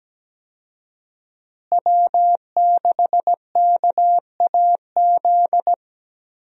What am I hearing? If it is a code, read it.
Morse code, heard as a keyed tone, about 17 wpm, 700 Hz. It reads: W6KAZ
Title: USA Call Signs - 5 Characters - Repeated 3x - 17wpm